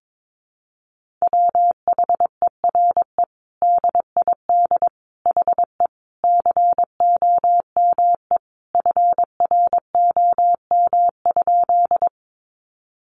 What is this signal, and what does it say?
Morse code, heard as a keyed tone, about 22 wpm, 700 Hz. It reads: WHERE DID HE COME FROM?
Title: Sentences from Top 200 Words 22wpm